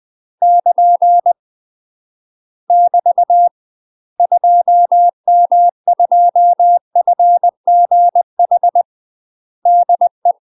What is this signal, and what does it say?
Morse code, heard as a keyed tone, about 20 wpm, 700 Hz. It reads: <KN> <BT> 2M2FGH DE
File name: UK MASTER QSOs TRAINING / 20WPM / UK MASTER QSOs 0008